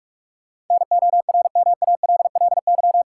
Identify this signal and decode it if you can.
Morse code, heard as a keyed tone, about 46 wpm, 700 Hz. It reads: NORMALLY
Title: Sentences from Top 1000 Words - Review Part 6 - New Word Drill - 45wpm